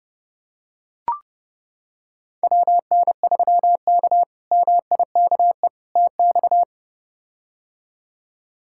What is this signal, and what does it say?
Morse code, heard as a keyed tone, about 31 wpm, 700 Hz. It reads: WN3K MIKE TX
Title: K1USN SST QSO Practice - 30wpm